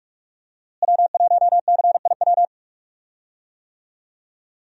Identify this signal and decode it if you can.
Morse code, heard as a keyed tone, about 45 wpm, 700 Hz. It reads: W1CIW
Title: USA Call Signs - 5 Characters - Repeated 2x - 45wpm